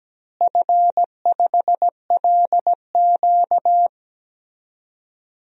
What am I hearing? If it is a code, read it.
Morse code, heard as a keyed tone, about 17 wpm, 700 Hz. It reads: F5LQ
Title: DXCC-100 Call Signs - 4 Characters - Repeated 3x - 17wpm